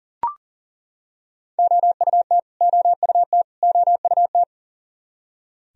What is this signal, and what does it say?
Morse code, heard as a keyed tone, about 40 wpm, 700 Hz. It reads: OUT OUT OUT
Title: Top 100 Words Repeated 40wpm